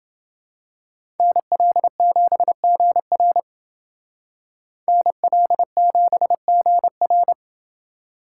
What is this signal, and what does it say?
Morse code, heard as a keyed tone, about 28 wpm, 700 Hz. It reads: NL7GR NL7GR